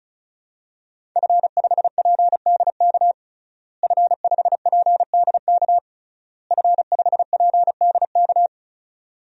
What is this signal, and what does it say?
Morse code, heard as a keyed tone, about 34 wpm, 700 Hz. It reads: F5PDK F5PDK F5PDK
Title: DXCC-100 Call Signs - 5 Characters - Repeated 3x - 35wpm